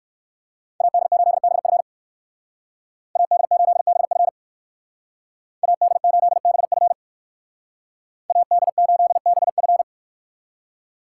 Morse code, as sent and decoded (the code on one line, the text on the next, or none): .- -.. ---.. -... ..-. / .- -.. ---.. -... ..-. / .- -.. ---.. -... ..-. / .- -.. ---.. -... ..-.
AD8BF AD8BF AD8BF AD8BF